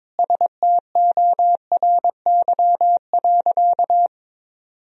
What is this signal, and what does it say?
Morse code, heard as a keyed tone, about 22 wpm, 700 Hz. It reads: STORY.
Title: Sentences from Top 300 Words and Abbreviations 22wpm